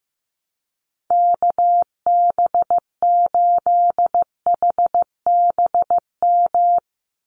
Morse code, heard as a keyed tone, about 15 wpm, 700 Hz. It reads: KB8HBM
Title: USA Call Signs 15wpm